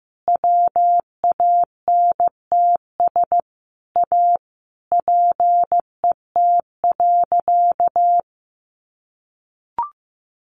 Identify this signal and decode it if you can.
Morse code, heard as a keyed tone, about 15 wpm, 700 Hz. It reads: WANTS A PET.
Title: Easy Phrases 15wpm